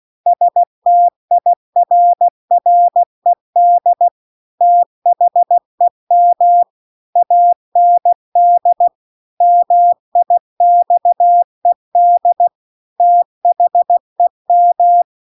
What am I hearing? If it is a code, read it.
Morse code, heard as a keyed tone, about 16 wpm, 700 Hz. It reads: STIRRED THEM AND MIXED THEM